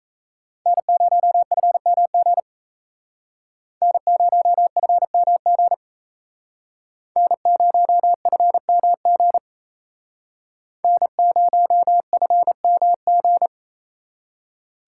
Morse code, heard as a keyed tone, about 34 wpm, 700 Hz. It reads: N0FMG N0FMG N0FMG N0FMG
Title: Speed Racing - USA 1x3 Call Signs 28wpm